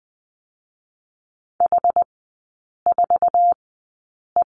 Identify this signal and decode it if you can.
Morse code, heard as a keyed tone, about 20 wpm, spaced as for 8 wpm, 700 Hz. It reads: H4E
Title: 074 Lesson 13e - Letter L - Three Characters 20wpm